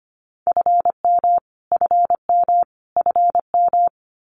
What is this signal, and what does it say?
Morse code, heard as a keyed tone, about 25 wpm, 700 Hz. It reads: FM FM FM
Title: Abbreviations Repeated 3x - 25wpm